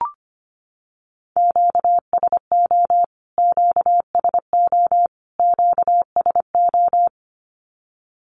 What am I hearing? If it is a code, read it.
Morse code, heard as a keyed tone, about 25 wpm, 700 Hz. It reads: QSO QSO QSO
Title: Q-Codes Repeated 3x 25wpm